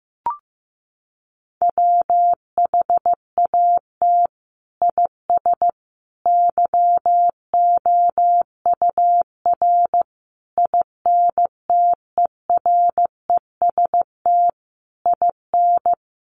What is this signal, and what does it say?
Morse code, heard as a keyed tone, about 15 wpm, 700 Hz. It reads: WHAT IS YOUR INTEREST IN
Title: Sentences from Top 1300 Words 15wpm